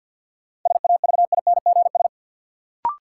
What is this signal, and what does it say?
Morse code, heard as a keyed tone, about 49 wpm, 700 Hz. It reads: SAVINGS
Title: Sentences from Top 1200 Words - New Word Drill Alt - 50wpm